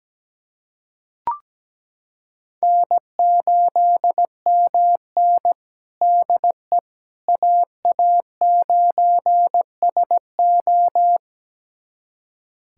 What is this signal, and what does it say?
Morse code, heard as a keyed tone, about 17 wpm, 700 Hz. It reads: N8MN DE AA9SO
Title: USA Callsign DE Callsign 17wpm